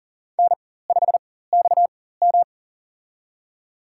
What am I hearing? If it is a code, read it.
Morse code, heard as a keyed tone, about 40 wpm, spaced as for 19 wpm, 700 Hz. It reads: N5XM
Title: USA Call Signs - 4 Characters - 4x Character Spacing - 40wpm